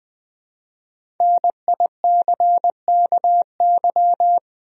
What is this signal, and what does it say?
Morse code, heard as a keyed tone, about 21 wpm, 700 Hz. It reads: NICKY
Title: USA Common Names - 20wpm